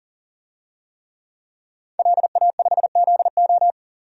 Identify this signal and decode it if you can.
Morse code, heard as a keyed tone, about 40 wpm, 700 Hz. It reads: LA5ZO
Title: DXCC-100 Call Signs - 5 Characters - 40wpm